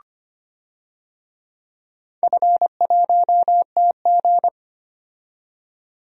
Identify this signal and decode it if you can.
Morse code, heard as a keyed tone, about 25 wpm, 700 Hz. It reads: F1TG
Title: DXCC-100 Call Signs - 4 Characters - 25wpm